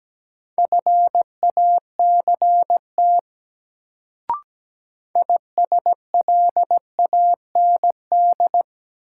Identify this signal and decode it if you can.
Morse code, heard as a keyed tone, about 17 wpm, 700 Hz. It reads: FACT ISLAND
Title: Sentences from Top 500 Words - New Word Drill - 17wpm